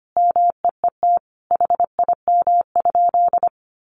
Morse code, heard as a keyed tone, about 25 wpm, 700 Hz. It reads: MEET HIM?
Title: Sentences from Top 1100 Words 25wpm